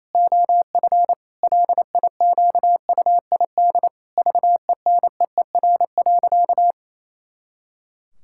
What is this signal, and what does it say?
Morse code, heard as a keyed tone, about 28 wpm, 700 Hz. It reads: OF LIQUID VENEER.
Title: Pangram Challenge v2 - 28wpm